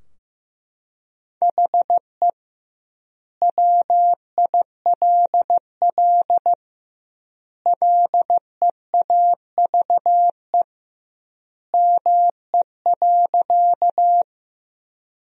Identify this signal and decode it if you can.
Morse code, heard as a keyed tone, about 15 wpm, 700 Hz. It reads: HE WILL LEAVE ME.
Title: 2x Word-Spacing - Sentences from Top 400 Words - 15wpm